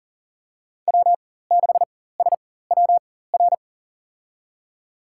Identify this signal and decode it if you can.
Morse code, heard as a keyed tone, about 37 wpm, spaced as for 18 wpm, 700 Hz. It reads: W6SWR
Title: USA Call Signs - 4x Character Spacing - 40wpm